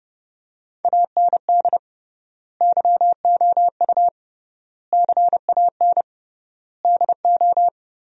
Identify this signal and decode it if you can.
Morse code, heard as a keyed tone, about 30 wpm, 700 Hz. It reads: AND YOU CAN DO
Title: 3x Word-Spacing - Sentences from Top 500 Words - 30wpm